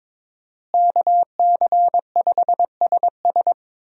Morse code, heard as a keyed tone, about 22 wpm, 700 Hz. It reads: KC5SS
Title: USA Call Signs - 5 Characters - Repeated 2x - 22wpm